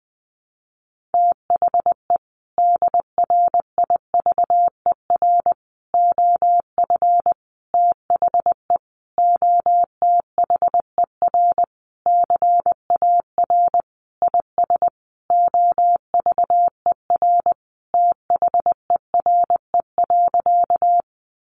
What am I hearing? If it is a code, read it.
Morse code, heard as a keyed tone, about 20 wpm, 700 Hz. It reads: THE DRIVER OF THE OTHER CAR IS OVER THERE.
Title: Sentences from Top 1000 Words 20wpm